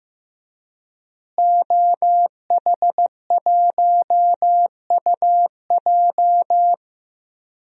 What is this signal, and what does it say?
Morse code, heard as a keyed tone, about 15 wpm, 700 Hz. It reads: OH1UJ
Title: DXCC-100 Call Signs - 5 Characters - Repeated 3x - 15wpm